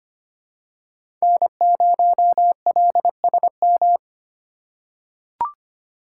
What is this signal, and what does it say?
Morse code, heard as a keyed tone, about 25 wpm, 700 Hz. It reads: N0LSM
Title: USA Call Signs - 5 Characters - 3x Character Spacing - 25wpm